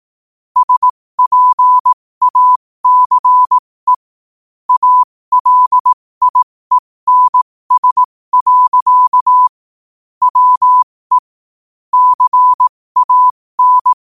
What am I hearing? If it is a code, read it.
Morse code, heard as a keyed tone, about 18 wpm, spaced as for 15 wpm, 1000 Hz. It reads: SPACE ALIENS. WE CAN
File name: qotdfiles / 15wpm